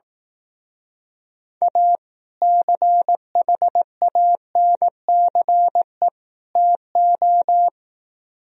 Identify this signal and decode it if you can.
Morse code, heard as a keyed tone, about 18 wpm, 700 Hz. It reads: A CHANCE TO
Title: Speed-Racing - Sets of 3 Words 15wpm